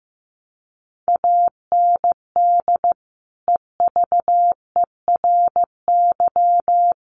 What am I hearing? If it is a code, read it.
Morse code, heard as a keyed tone, about 14 wpm, 700 Hz. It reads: AND EVERY
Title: Sentences from Top 1200 Words 15wpm